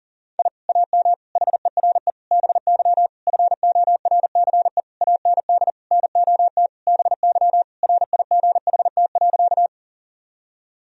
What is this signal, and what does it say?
Morse code, heard as a keyed tone, about 40 wpm, 700 Hz. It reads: I AM HERE BY FORCE AND NOT BY RIGHT.
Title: Sentences from Top 1000 Words - Review Part 3 - 40wpm